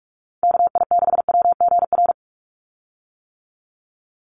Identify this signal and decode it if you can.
Morse code, heard as a keyed tone, about 44 wpm, 700 Hz. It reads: KI6WGR
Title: USA Call Signs 45wpm